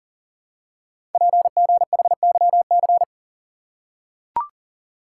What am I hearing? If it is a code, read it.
Morse code, heard as a keyed tone, about 41 wpm, 700 Hz. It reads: PGHYC
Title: Code Groups - 5 Letters - 40wpm